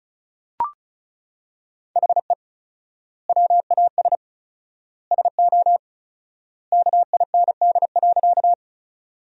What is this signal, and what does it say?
Morse code, heard as a keyed tone, about 36 wpm, 700 Hz. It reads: HE WAS SO KIND.
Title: 4x Word-Spacing - Sentences from Top 200 Words - 35wpm